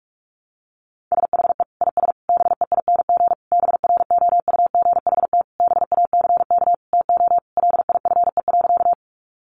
Morse code, heard as a keyed tone, about 44 wpm, 700 Hz. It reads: SHE IS BEING BROUGHT BACK TO LIFE.
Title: Sentences from Top 600 Words 45wpm